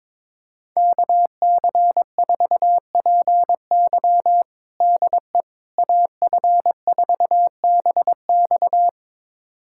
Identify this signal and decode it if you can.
Morse code, heard as a keyed tone, about 22 wpm, 700 Hz. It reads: KC4PY DE AF4BX
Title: USA Callsign DE Callsign - 5 Characters - 22wpm